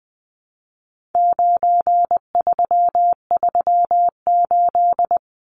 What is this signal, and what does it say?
Morse code, heard as a keyed tone, about 20 wpm, 700 Hz. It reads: 9338
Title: Numbers - 4-Digit 20wpm